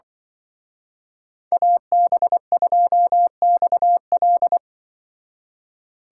Morse code, heard as a keyed tone, about 24 wpm, 700 Hz. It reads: AB2XL
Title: Speed Racing - USA 2x2 Call Signs 20wpm